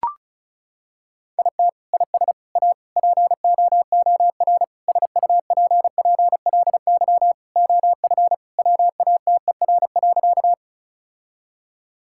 Morse code, heard as a keyed tone, about 36 wpm, 700 Hz. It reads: IT IS A POOR SUPPLY OF WATER.
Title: Sentences from Top 1000 Words - Review Part 5 - 35wpm